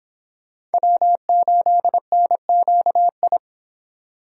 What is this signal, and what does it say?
Morse code, heard as a keyed tone, about 30 wpm, 700 Hz. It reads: W8NQI